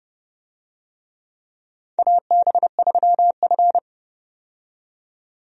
Morse code, heard as a keyed tone, about 30 wpm, 700 Hz. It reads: AB3F
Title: USA Call Signs - 4 Characters - 30wpm